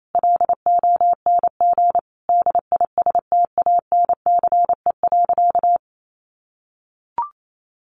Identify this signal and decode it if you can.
Morse code, heard as a keyed tone, about 28 wpm, 700 Hz. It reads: LONG DISTANCE.
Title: Sentences from Top 1200 Words 28wpm